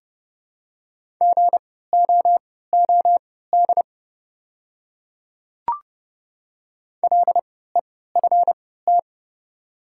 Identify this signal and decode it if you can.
Morse code, heard as a keyed tone, about 30 wpm, spaced as for 17 wpm, 690 Hz. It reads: GOOD LEFT
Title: 3x Character Spacing - Top 300 Words - 30wpm